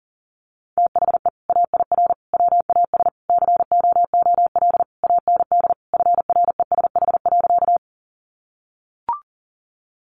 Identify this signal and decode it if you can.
Morse code, heard as a keyed tone, about 40 wpm, 700 Hz. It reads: THE AIR WAS COOL AND FRESH.
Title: Sentences from Top 1200 Words 40wpm